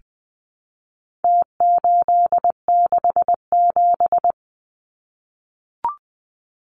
Morse code, heard as a keyed tone, about 20 wpm, 700 Hz. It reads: T867
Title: Numbers - Contest Style - 4-Digit 20wpm